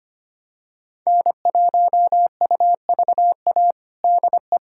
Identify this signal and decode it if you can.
Morse code, heard as a keyed tone, about 25 wpm, 700 Hz. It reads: N1UVA DE
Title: USA Callsign DE Callsign - 5 Characters - 25wpm